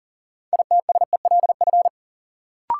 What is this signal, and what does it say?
Morse code, heard as a keyed tone, about 40 wpm, 700 Hz. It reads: ITSELF
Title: Sentences from Top 800 Words - New Word Drill - 40wpm